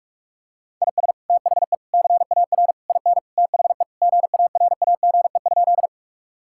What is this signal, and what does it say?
Morse code, heard as a keyed tone, about 46 wpm, 700 Hz. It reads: IS THE CAR IN THE GARAGE?
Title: Sentences from Top 1600 Words 45wpm